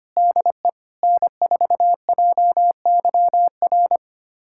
Morse code, heard as a keyed tone, about 25 wpm, 700 Hz. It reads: DE N4JYR
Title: USA Callsign DE Callsign - 5 Characters - 25wpm